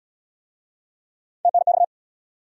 Morse code, heard as a keyed tone, about 69 wpm, 700 Hz. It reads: TNX